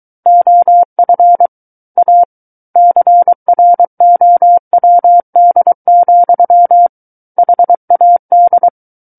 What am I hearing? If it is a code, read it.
Morse code, heard as a keyed tone, about 23 wpm, 700 Hz. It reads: OF A CROWD, HAD
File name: Book One / 14-Chapter 14 - 23 WPM